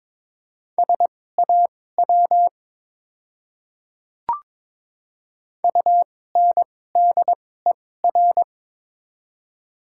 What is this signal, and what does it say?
Morse code, heard as a keyed tone, about 21 wpm, spaced as for 16 wpm, 700 Hz. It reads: SAW UNDER
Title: 2x Character Spacing - Top 300 Words - 22wpm